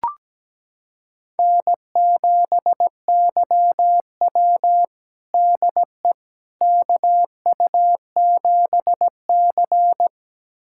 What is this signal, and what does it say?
Morse code, heard as a keyed tone, about 17 wpm, 700 Hz. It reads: N7YW DE KU7C
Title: USA Callsign DE Callsign - 4 Characters - 17wpm